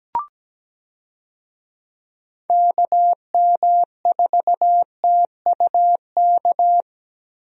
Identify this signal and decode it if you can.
Morse code, heard as a keyed tone, about 17 wpm, 700 Hz. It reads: KM4TUK
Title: USA Call Signs - 6 Characters - Repeated 3x - 17wpm